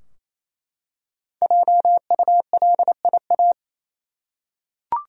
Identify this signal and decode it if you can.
Morse code, heard as a keyed tone, about 27 wpm, 700 Hz. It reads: JULIA
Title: USA Common Names - Max 5 Letters - 28wpm